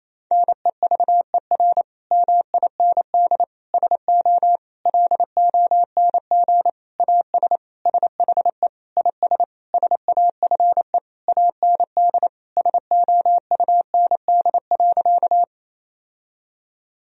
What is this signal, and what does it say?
Morse code, heard as a keyed tone, about 28 wpm, 700 Hz. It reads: NEVER MIND SO LONG AS SHE IS SAFE AND SOUND.
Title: Sentences from Top 1000 Words - Review Part 1 - 28wpm